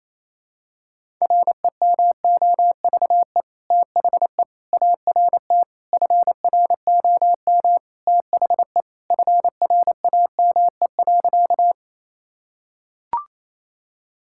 Morse code, heard as a keyed tone, about 28 wpm, 700 Hz. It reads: REMOVE THE ART FROM THE FRAME.
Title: Sentences from Top 1000 Words - Review Part 6 - 28wpm